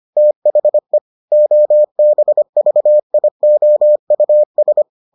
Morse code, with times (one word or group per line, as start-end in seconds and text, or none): 0.2-1.0 THE
1.3-4.8 OBVIOUS